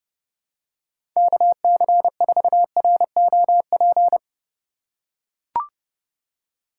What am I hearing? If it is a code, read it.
Morse code, heard as a keyed tone, about 31 wpm, 700 Hz. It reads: KC4ROP
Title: USA Call Signs - 6 Characters - 2x Character Spacing - 30wpm